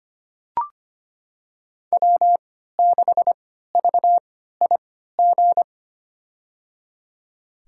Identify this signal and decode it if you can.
Morse code, heard as a keyed tone, about 25 wpm, spaced as for 14 wpm, 700 Hz. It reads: W6VIG
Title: USA Call Signs - 5 Characters - 3x Character Spacing - 25wpm